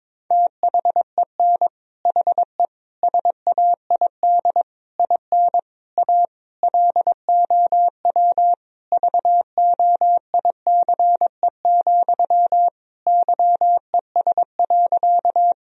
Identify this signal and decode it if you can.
Morse code, heard as a keyed tone, about 22 wpm, 700 Hz. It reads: THEN HE SAID IN A LOW VOICE, YES.
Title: Sentences from Top 1000 Words - Review Part 1 - 22wpm